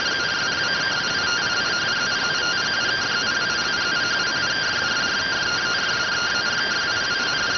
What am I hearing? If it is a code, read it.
Two-tone teleprinter signal, about 170 Hz shift.